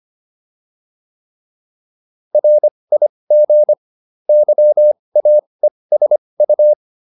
Morse code, heard as a keyed tone, about 25 wpm, spaced as for 20 wpm, 600 Hz. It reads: RIG YAESU